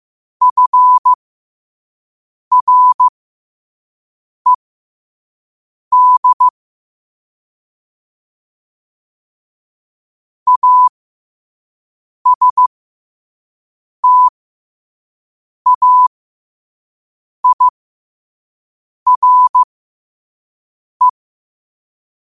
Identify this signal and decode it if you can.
Morse code, heard as a keyed tone, about 15 wpm, spaced as for 5 wpm, 1000 Hz. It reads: FRED ASTAIRE
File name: qotdfiles / 5wpm